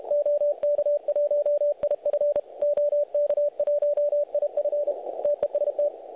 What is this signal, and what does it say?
Morse code, heard as a keyed tone, about 32 wpm, 580 Hz, with radio noise. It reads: OK1IF OK1IF TEST
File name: truSDX / websdr recording 3541.9kHz